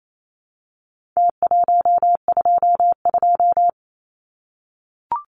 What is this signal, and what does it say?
Morse code, heard as a keyed tone, about 28 wpm, 700 Hz. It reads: T122
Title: Numbers - Contest Style - 4-Digit 28wpm